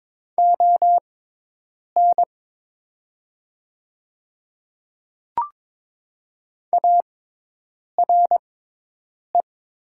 Morse code, heard as a keyed tone, about 22 wpm, spaced as for 8 wpm, 700 Hz. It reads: ON ARE
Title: 6x Character Spacing - Top 300 Words - 22wpm